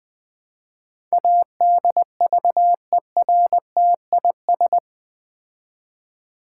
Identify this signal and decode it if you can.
Morse code, heard as a keyed tone, about 20 wpm, 700 Hz. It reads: ADVERTIS
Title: Build A Word - 4 Letters At A Time - 11-Letter Words - 20wpm